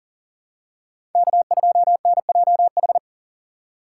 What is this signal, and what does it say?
Morse code, heard as a keyed tone, about 40 wpm, 700 Hz. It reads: K2NJH
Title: USA Call Signs - 2x Character Spacing - 40wpm